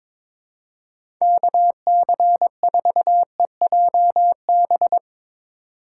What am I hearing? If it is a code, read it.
Morse code, heard as a keyed tone, about 21 wpm, 700 Hz. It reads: KC4EJB
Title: USA Call Signs - 2x Character Spacing - 22wpm